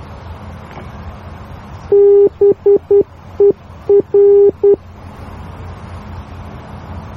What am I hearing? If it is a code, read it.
Morse code, heard as a keyed tone, about 10 wpm, 400 Hz. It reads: BER